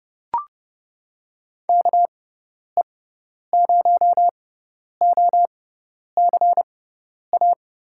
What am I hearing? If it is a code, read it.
Morse code, heard as a keyed tone, about 30 wpm, spaced as for 10 wpm, 700 Hz. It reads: KE0OCA
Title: USA Call Signs - 6 Characters - 6x Character Spacing - 30wpm